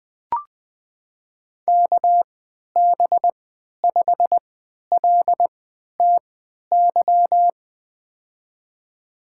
Morse code, heard as a keyed tone, about 20 wpm, spaced as for 11 wpm, 700 Hz. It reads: KB5LTY
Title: USA Call Signs - 6 Characters - 3x Character Spacing - 20wpm